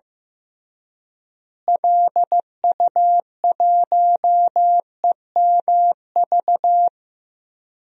Morse code, heard as a keyed tone, about 15 wpm, 700 Hz. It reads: LU1EMV
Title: DXCC-100 Call Signs - 6 Characters - Repeated 2x - 15wpm